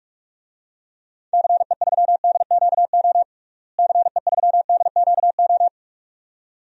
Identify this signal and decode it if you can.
Morse code, heard as a keyed tone, about 46 wpm, 700 Hz. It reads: CE3DQO CE3DQO
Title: DXCC-100 Call Signs - 6 Characters - Repeated 2x - 45wpm